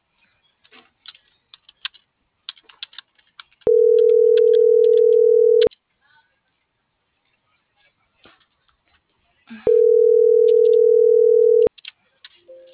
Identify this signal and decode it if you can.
Ringback tone (North America), 440 + 480 Hz.